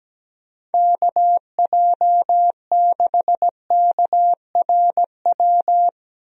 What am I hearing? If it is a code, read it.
Morse code, heard as a keyed tone, about 17 wpm, 700 Hz. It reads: KJ6KRW
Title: USA Call Signs - 6 Characters - Repeated 3x - 17wpm